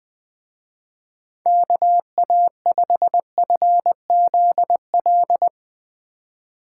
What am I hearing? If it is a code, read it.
Morse code, heard as a keyed tone, about 21 wpm, 700 Hz. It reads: KA5FZL